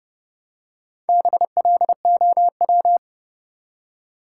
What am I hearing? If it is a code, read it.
Morse code, heard as a keyed tone, about 32 wpm, 700 Hz. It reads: BLOW
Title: Four Letter Words - 30wpm